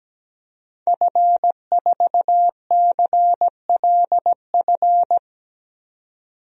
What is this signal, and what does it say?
Morse code, heard as a keyed tone, about 17 wpm, 700 Hz. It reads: F4CLF
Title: DXCC-100 Call Signs - 5 Characters - 17wpm